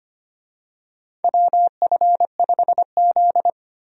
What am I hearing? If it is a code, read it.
Morse code, heard as a keyed tone, about 25 wpm, 700 Hz. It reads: WF5Z
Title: USA Call Signs - 4 Characters - Repeated 3x - 25wpm